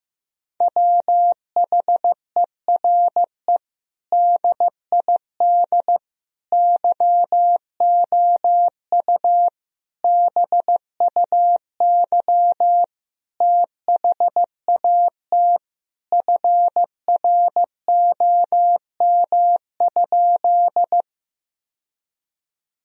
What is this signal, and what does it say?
Morse code, heard as a keyed tone, about 15 wpm, 700 Hz. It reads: WHERE DID YOU BUY THAT FROM?
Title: Sentences from Top 1000 Words - Review Part 4 - 15wpm